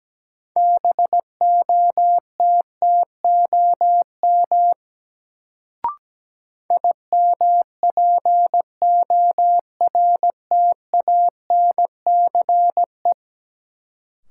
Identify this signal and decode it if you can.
Morse code, heard as a keyed tone, about 17 wpm, 700 Hz. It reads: BOTTOM IMPORTANCE
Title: Sets of 3 Words - New Word Drill - 17wpm